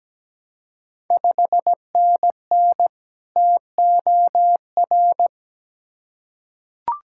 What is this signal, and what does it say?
Morse code, heard as a keyed tone, about 17 wpm, 700 Hz. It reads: 5NN TOR
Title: Ontario QSO Party Exchange 17wpm